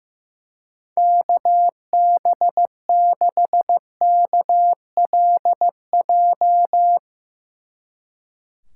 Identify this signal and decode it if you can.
Morse code, heard as a keyed tone, about 15 wpm, 700 Hz. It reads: KB6KLJ
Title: Speed Racing - USA 2x3 Call Signs 15wpm